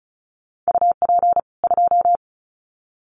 Morse code, heard as a keyed tone, about 34 wpm, 700 Hz. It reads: UP 2
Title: Common QSO Phrases 35wpm